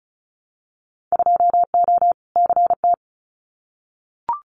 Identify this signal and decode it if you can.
Morse code, heard as a keyed tone, about 36 wpm, 690 Hz. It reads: 2O CT
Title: Winter Field Day Exchanges 35wpm